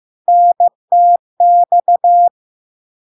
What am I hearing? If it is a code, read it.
Morse code, heard as a keyed tone, about 15 wpm, 700 Hz. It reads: NTX
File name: LARK-FD-2022-Log-in-CW-15wpm-700-Hz